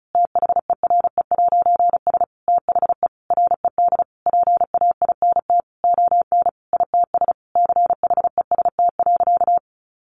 Morse code, heard as a keyed tone, about 34 wpm, 700 Hz. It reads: THERE'S THE RED PAINT ON ITS CHEST.
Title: A Sound Of Thunder 35wpm